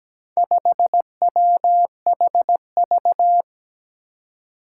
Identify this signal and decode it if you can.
Morse code, heard as a keyed tone, about 17 wpm, 700 Hz. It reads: 5WHV
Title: Four Challenging Letter-Number Combinations 17wpm